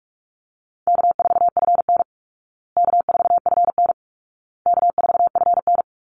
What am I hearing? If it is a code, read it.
Morse code, heard as a keyed tone, about 44 wpm, 700 Hz. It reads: K4FN K4FN K4FN
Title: Call Signs Repeated 2x - 45wpm